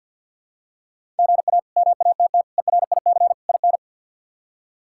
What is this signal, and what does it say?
Morse code, heard as a keyed tone, about 48 wpm, 700 Hz. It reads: GA MATT ERIC IN
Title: K1USN SST QSO Practice - 50wpm